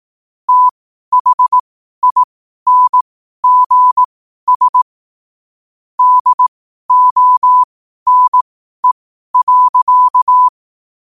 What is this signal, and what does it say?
Morse code, heard as a keyed tone, about 18 wpm, spaced as for 12 wpm, 1000 Hz. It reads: THINGS DONE.
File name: qotdfiles / 12wpm